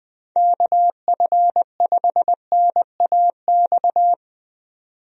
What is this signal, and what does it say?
Morse code, heard as a keyed tone, about 20 wpm, 700 Hz. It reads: KF5NAX
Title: USA Call Signs - 6 Characters - 20wpm